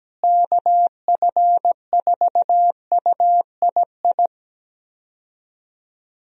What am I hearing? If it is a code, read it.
Morse code, heard as a keyed tone, about 17 wpm, 700 Hz. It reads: KF4UII
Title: USA Call Signs - 6 Characters - Repeated 2x - 17wpm